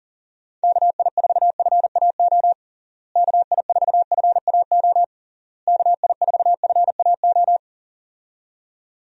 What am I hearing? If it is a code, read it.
Morse code, heard as a keyed tone, about 40 wpm, 700 Hz. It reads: KI4FAO KI4FAO KI4FAO
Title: USA Call Signs - 6 Characters - Repeated 3x - 40wpm